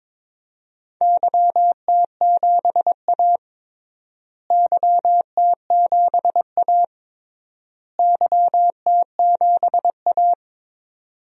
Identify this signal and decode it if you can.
Morse code, heard as a keyed tone, about 22 wpm, 700 Hz. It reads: YT7A YT7A YT7A
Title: DXCC-100 Call Signs - 4 Characters - Repeated 3x - 22wpm